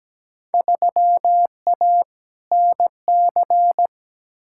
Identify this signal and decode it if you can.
Morse code, heard as a keyed tone, about 17 wpm, 700 Hz. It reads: 3A NC
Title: Field Day Exchanges Encore - 17wpm